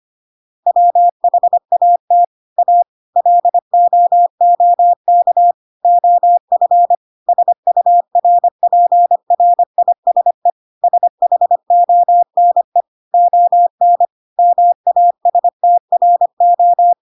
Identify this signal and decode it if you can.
Morse code, heard as a keyed tone, about 25 wpm, 700 Hz. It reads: WHAT A LOOK OF SURPRISE SHONE ON MASTRO
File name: Pinocchio / Pinocchio ch1 25 25 ik1hge